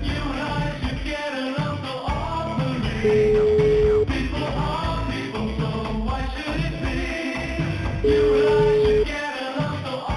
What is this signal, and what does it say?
Ringback tone (Europe, 425 Hz), with noise.